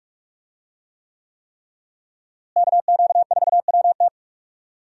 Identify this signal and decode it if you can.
Morse code, heard as a keyed tone, about 44 wpm, 700 Hz. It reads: KQ4WT
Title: USA Call Signs - 5 Characters - Repeated 3x - 45wpm